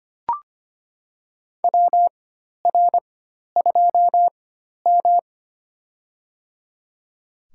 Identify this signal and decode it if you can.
Morse code, heard as a keyed tone, about 25 wpm, spaced as for 12 wpm, 700 Hz. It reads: WR2M
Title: USA Call Signs - 4x Character Spacing - 25wpm